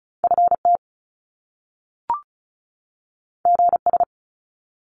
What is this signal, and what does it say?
Morse code, heard as a keyed tone, about 36 wpm, 700 Hz. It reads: FT GS
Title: Top Two Letter Combinations in Words 35wpm